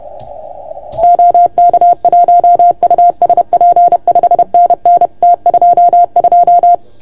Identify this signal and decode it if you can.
Morse code, heard as a keyed tone, about 30 wpm, 690 Hz. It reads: OK1USP 5NN T22
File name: Archiv / CUC381 / OK1FGD CUC 381